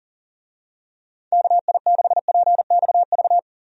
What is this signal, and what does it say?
Morse code, heard as a keyed tone, about 40 wpm, 700 Hz. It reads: KI6PXV
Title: USA Call Signs - 6 Characters - 40wpm